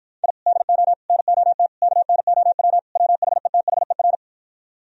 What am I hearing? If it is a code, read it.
Morse code, heard as a keyed tone, about 56 wpm, 700 Hz. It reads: I DO NOT KNOW WHETHER